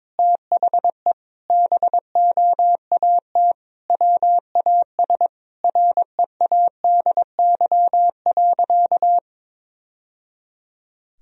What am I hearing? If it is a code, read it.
Morse code, heard as a keyed tone, about 22 wpm, 700 Hz. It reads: THE BOAT WAS READY.
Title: Sentences from Top 1000 Words - Review Part 3 - 22wpm